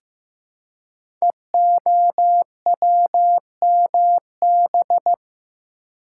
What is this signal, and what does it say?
Morse code, heard as a keyed tone, about 15 wpm, 700 Hz. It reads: EOWMB
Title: Code Groups - 5 Letter:Numbers - 15wpm